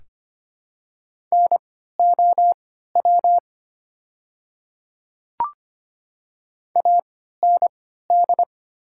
Morse code, heard as a keyed tone, about 25 wpm, spaced as for 14 wpm, 700 Hz. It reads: NOW AND
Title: 3x Character Spacing - Top 100 Words - 25wpm